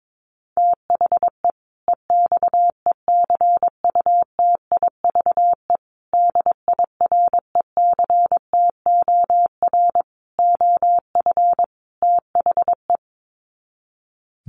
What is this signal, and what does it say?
Morse code, heard as a keyed tone, about 22 wpm, 700 Hz. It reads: THE EXECUTIVE DIRECTOR OF THE
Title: Sets of 5 Words Encore 22wpm